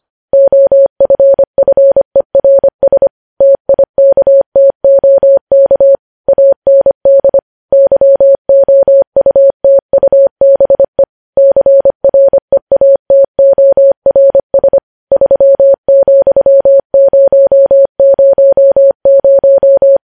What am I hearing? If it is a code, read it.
Morse code, heard as a keyed tone, about 25 wpm, 550 Hz. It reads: OFFERS TIKTOK AND YOUTUBE CREATORS 3,000
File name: podcast / News headlines-1x-25x25